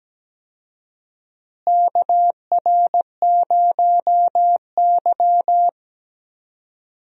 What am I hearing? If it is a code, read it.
Morse code, heard as a keyed tone, about 17 wpm, 700 Hz. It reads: KR0Y
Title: USA Call Signs - 4 Characters - Repeated 2x - 17wpm